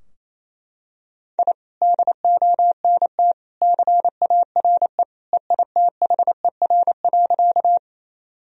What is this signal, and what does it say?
Morse code, heard as a keyed tone, about 28 wpm, 700 Hz. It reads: I DONT CARE EITHER.
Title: Sentences from Top 1000 Words - Review Part 4 - 28wpm